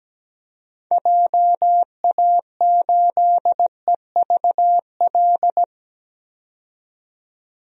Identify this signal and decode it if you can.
Morse code, heard as a keyed tone, about 17 wpm, 700 Hz. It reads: JA8EVL
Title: DXCC-100 Call Signs - 6 Characters - 17wpm